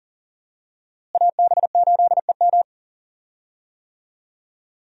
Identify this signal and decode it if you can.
Morse code, heard as a keyed tone, about 40 wpm, 700 Hz. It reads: AB8EM
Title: USA Call Signs - 5 Characters - Repeated 2x - 40wpm